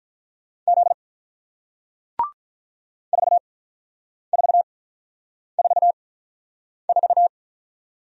Morse code, heard as a keyed tone, about 48 wpm, spaced as for 9 wpm, 700 Hz. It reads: Z 4444